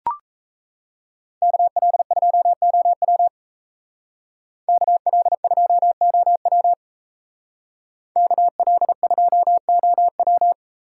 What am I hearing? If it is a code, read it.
Morse code, heard as a keyed tone, about 34 wpm, 700 Hz. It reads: KL2OW KL2OW KL2OW
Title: Speed Racing - USA Call Signs 28wpm